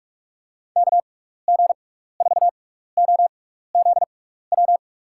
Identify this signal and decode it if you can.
Morse code, heard as a keyed tone, about 46 wpm, spaced as for 15 wpm, 700 Hz. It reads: KG4OZW
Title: USA Call Signs - 6x Character Spacing - 45wpm